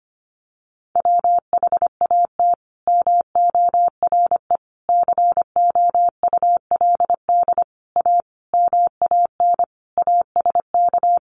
Morse code, heard as a keyed tone, about 25 wpm, 700 Hz. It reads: WHAT MORE COULD A MAN ASK
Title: Sentences from Top 200 Words 25wpm